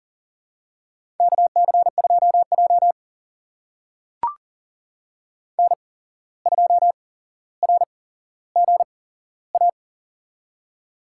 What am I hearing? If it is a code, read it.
Morse code, heard as a keyed tone, about 40 wpm, spaced as for 16 wpm, 700 Hz. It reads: KC2J N2RGA
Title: USA Call Signs - 8x Character Spacing - 40wpm